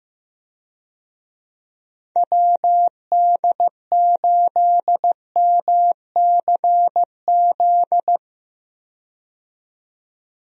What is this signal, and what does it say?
Morse code, heard as a keyed tone, about 15 wpm, 700 Hz. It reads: WD8MCZ
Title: USA Call Signs - 6 Characters - 15wpm